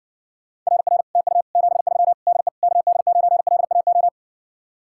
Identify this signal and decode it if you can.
Morse code, heard as a keyed tone, about 70 wpm, 700 Hz. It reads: RR TU 73 DE KD9LAO